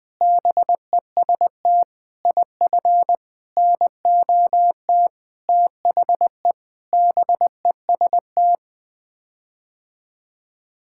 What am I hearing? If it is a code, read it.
Morse code, heard as a keyed tone, about 20 wpm, 700 Hz. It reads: BEST IF NOT THE BEST
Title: Sets of 8 Words - 20wpm